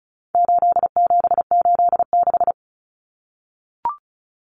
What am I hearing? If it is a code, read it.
Morse code, heard as a keyed tone, about 34 wpm, 700 Hz. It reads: 8786
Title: Numbers - 4-Digit 35wpm